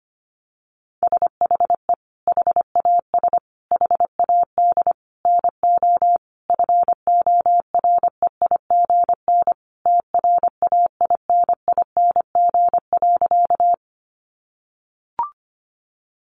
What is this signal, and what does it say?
Morse code, heard as a keyed tone, about 25 wpm, 700 Hz. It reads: SHE HAS HAD NO FOREIGN TRAINING.
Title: Sentences from Top 1400 Words 25wpm